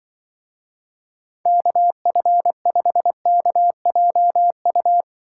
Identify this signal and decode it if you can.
Morse code, heard as a keyed tone, about 24 wpm, 700 Hz. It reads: KF5KJU
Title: Speed Racing - USA 2x3 Call Signs 20wpm